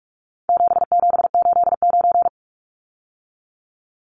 Morse code, heard as a keyed tone, about 44 wpm, 700 Hz. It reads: 7789
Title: Numbers - 4-Digit 45wpm